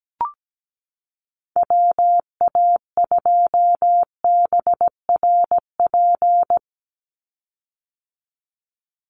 Morse code, heard as a keyed tone, about 17 wpm, 700 Hz. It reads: WA2BRP
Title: USA Call Signs 17wpm